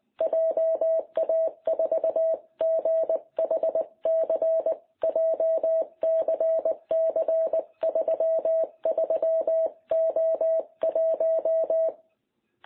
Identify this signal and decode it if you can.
Morse code, heard as a keyed tone, about 19 wpm, 660 Hz. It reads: JA4GHC JCC33O1
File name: jh8cbh / 2021cw